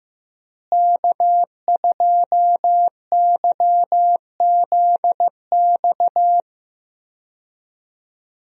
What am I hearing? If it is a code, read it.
Morse code, heard as a keyed tone, about 15 wpm, 700 Hz. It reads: K2YZX
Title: USA Call Signs - 5 Characters - Repeated 3x - 15wpm